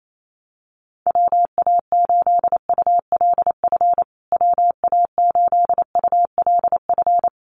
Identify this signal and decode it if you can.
Morse code, heard as a keyed tone, about 28 wpm, 700 Hz. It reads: WA8ULF WA8ULF
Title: USA Call Signs Repeated 2x 28wpm